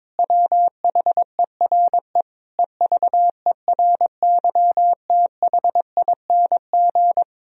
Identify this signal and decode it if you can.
Morse code, heard as a keyed tone, about 22 wpm, 700 Hz. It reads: WHERE EVERYTHING
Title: Sentences from Top 1000 Words - Review Part 4 - 22wpm